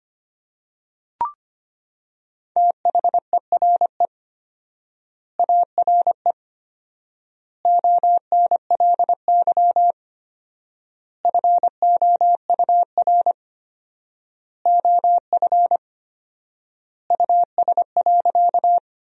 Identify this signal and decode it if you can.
Morse code, heard as a keyed tone, about 25 wpm, 700 Hz. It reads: THERE ARE ONLY FOUR OF US.
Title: 4x Word-Spacing - Sentences from Top 300 Words - 25wpm